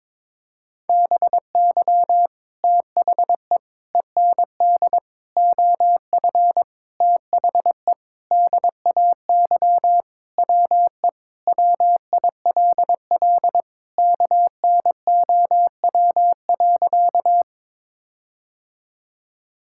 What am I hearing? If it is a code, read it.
Morse code, heard as a keyed tone, about 22 wpm, 700 Hz. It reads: BY THE END OF THE DAY WE WILL KNOW.
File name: practice / 22wpm / Sentences with 4-Word Phrases - 22wpm